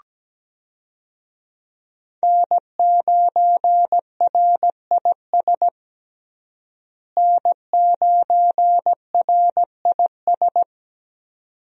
Morse code, heard as a keyed tone, about 17 wpm, 700 Hz. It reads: N9RIS N9RIS
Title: USA Call Signs - 5 Characters - Repeated 2x - 17wpm